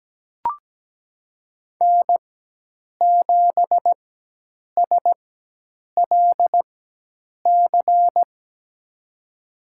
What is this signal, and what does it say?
Morse code, heard as a keyed tone, about 17 wpm, spaced as for 8 wpm, 700 Hz. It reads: N7SLC
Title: USA Call Signs - 4x Character Spacing - 17wpm